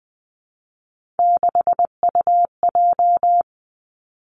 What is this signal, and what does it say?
Morse code, heard as a keyed tone, about 20 wpm, 700 Hz. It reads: 6UJ
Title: Three Challenging Letter-Number Combinations 20wpm